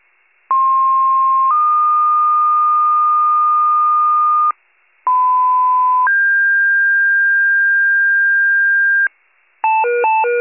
Two-tone page, heard as a one-second tone, then a three-second tone, about 1060 then 1220 Hz, about 1010 then 1600 Hz, with noise. It is heard from 0.5 to 9.1 s.
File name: pager / 94 RESCUE DUTY / wcalert